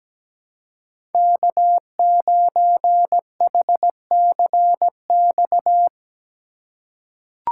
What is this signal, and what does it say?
Morse code, heard as a keyed tone, about 17 wpm, 700 Hz. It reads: K9HCX
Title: USA Call Signs - 5 Characters - 4x Character Spacing - 17wpm